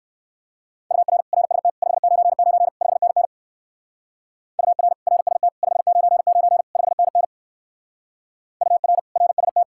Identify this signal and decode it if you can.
Morse code, heard as a keyed tone, about 58 wpm, 700 Hz. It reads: UR RST 599 5NN UR RST 599 5NN UR RST